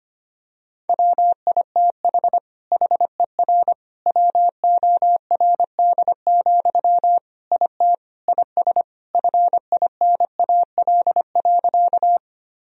Morse code, heard as a keyed tone, about 25 wpm, 700 Hz. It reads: WITH HER WORD, IT IS FINAL.
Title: Sentences from Top 1000 Words - Review Part 2 - 25wpm